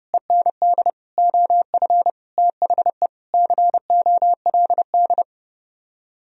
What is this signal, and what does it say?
Morse code, heard as a keyed tone, about 30 wpm, 700 Hz. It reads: END OF THE COLD